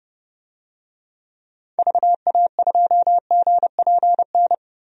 Morse code, heard as a keyed tone, about 30 wpm, 700 Hz. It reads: VA2GPN
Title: DXCC-100 Call Signs - 6 Characters - 30wpm